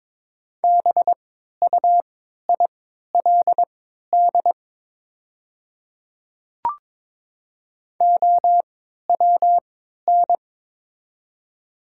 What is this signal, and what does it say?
Morse code, heard as a keyed tone, about 22 wpm, spaced as for 12 wpm, 690 Hz. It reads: BUILD OWN
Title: 3x Character Spacing - Top 300 Words - 22wpm